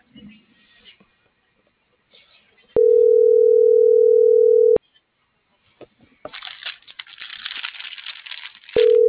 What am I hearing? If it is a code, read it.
Ringback tone (North America), 440 + 480 Hz.